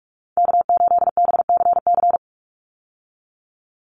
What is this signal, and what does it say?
Morse code, heard as a keyed tone, about 44 wpm, 700 Hz. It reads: K8BCC
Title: USA Call Signs 45wpm